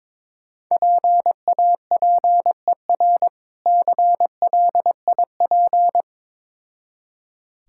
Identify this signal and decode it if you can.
Morse code, heard as a keyed tone, about 22 wpm, 700 Hz. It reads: PAPER CLIP
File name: Difficult Words with Hints - 22wpm